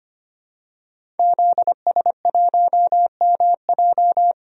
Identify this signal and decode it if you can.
Morse code, heard as a keyed tone, about 25 wpm, 700 Hz. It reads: ZS1MJ
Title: DXCC-100 Call Signs - 5 Characters - Repeated 2x - 25wpm